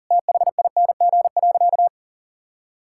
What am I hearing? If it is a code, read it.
Morse code, heard as a keyed tone, about 40 wpm, 700 Hz. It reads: THING.
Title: Sentences from Top 1000 Words - Review Part 5 - 40wpm